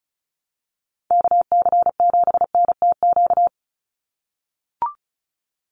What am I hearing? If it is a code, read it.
Morse code, heard as a keyed tone, about 34 wpm, 700 Hz. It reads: KC7NTQ
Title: USA Call Signs 35wpm